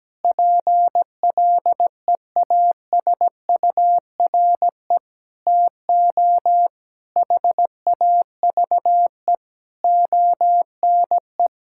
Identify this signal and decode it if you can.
Morse code, heard as a keyed tone, about 17 wpm, 700 Hz. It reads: PLEASURE TO HAVE ONE
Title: Sentences from Top 1600 Words 17wpm